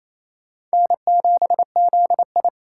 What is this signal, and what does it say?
Morse code, heard as a keyed tone, about 34 wpm, 700 Hz. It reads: N7ZI
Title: Speed Racing - USA 1x2 Call Signs 28wpm